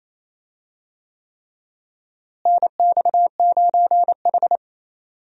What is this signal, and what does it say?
Morse code, heard as a keyed tone, about 27 wpm, 700 Hz. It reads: NX9H
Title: USA Call Signs - 4 Characters - Repeated 3x - 28wpm